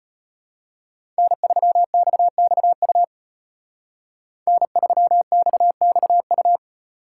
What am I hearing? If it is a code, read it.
Morse code, heard as a keyed tone, about 33 wpm, 700 Hz. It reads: N3XXU N3XXU